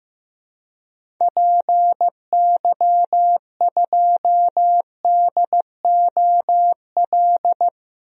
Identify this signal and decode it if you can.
Morse code, heard as a keyed tone, about 15 wpm, 700 Hz. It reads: PY2DOL
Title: DXCC-100 Call Signs - 6 Characters - Repeated 2x - 15wpm